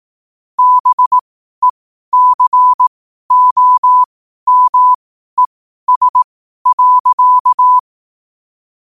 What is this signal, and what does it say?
Morse code, heard as a keyed tone, about 18 wpm, spaced as for 12 wpm, 1000 Hz. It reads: BECOMES.
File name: qotdfiles / 12wpm